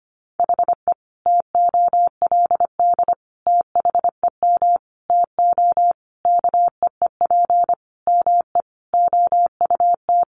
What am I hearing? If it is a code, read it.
Morse code, heard as a keyed tone, about 25 wpm, 700 Hz. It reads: HE TOLD THEM TO KEEP ME OUT
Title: Sentences from Top 1200 Words 25wpm